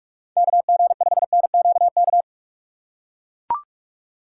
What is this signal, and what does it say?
Morse code, heard as a keyed tone, about 42 wpm, 700 Hz. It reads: KG5NQK
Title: USA Call Signs - 6 Characters - 4x Character Spacing - 45wpm